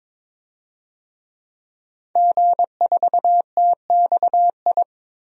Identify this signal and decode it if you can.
Morse code, heard as a keyed tone, about 21 wpm, 700 Hz. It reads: G4TXI
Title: DXCC-100 Call Signs - 5 Characters - 22wpm